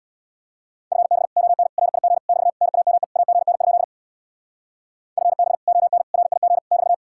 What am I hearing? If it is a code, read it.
Morse code, heard as a keyed tone, about 63 wpm, 700 Hz. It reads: FB ON RIG <BT> NAME AGN? FB ON RIG <BT>